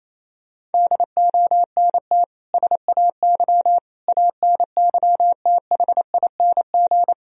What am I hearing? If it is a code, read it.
Morse code, heard as a keyed tone, about 28 wpm, 700 Hz. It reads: DONT SAY ANYTHING
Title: Sentences from Top 1000 Words - Review Part 1 - 28wpm